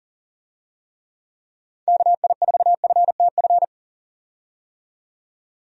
Morse code, heard as a keyed tone, about 40 wpm, 700 Hz. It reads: KI4FTF
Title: USA Call Signs - 6 Characters - 40wpm